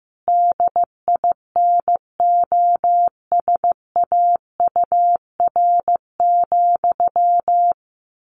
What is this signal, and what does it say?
Morse code, heard as a keyed tone, about 15 wpm, 700 Hz. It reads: DINOSAUR,
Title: A Sound Of Thunder 15wpm